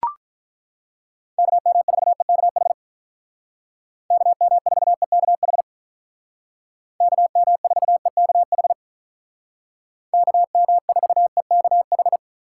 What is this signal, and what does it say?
Morse code, heard as a keyed tone, about 43 wpm, 700 Hz. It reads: KM4EKH KM4EKH KM4EKH KM4EKH
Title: Speed Racing - USA Call Signs 35wpm